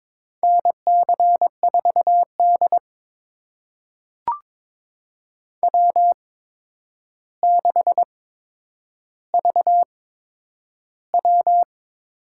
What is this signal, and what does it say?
Morse code, heard as a keyed tone, about 22 wpm, spaced as for 9 wpm, 700 Hz. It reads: NC4D W6VW
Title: USA Call Signs - 4 Characters - 8x Character Spacing - 22wpm